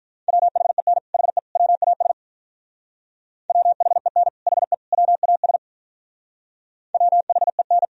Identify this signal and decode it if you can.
Morse code, heard as a keyed tone, about 46 wpm, 700 Hz. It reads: WHEN HE WAS WHEN HE WAS WHEN